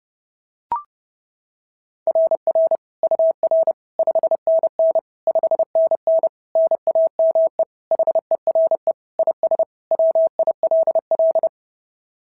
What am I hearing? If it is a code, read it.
Morse code, heard as a keyed tone, about 30 wpm, 690 Hz. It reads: RR UR 5NN 5NN NAME HERE IS WILL
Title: SOTA QSO Practice 30wpm